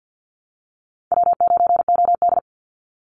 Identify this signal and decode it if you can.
Morse code, heard as a keyed tone, about 50 wpm, 700 Hz. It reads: W9OD
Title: Call Signs Repeated 3x - 50wpm